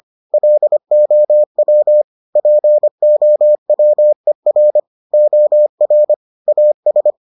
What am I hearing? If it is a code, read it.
Morse code, heard as a keyed tone, about 25 wpm, 600 Hz. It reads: LOW POWER OR AS